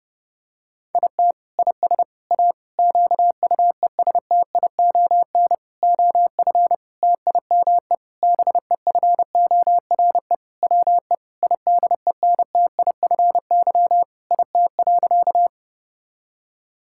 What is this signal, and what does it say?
Morse code, heard as a keyed tone, about 30 wpm, 700 Hz. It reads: IT IS A QUESTION OF TIME BEFORE WE IDENTIFY IT.
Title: Sentences from Top 1000 Words - Review Part 6 - 30wpm